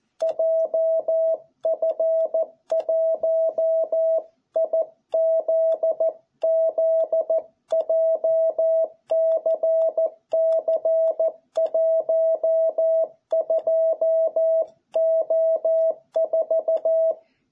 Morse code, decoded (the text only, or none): JF1IZZ JCC12O4